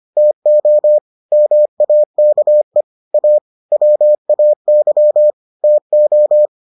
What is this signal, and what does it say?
Morse code, heard as a keyed tone, about 25 wpm, 600 Hz. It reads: TO MAKE A WAY TO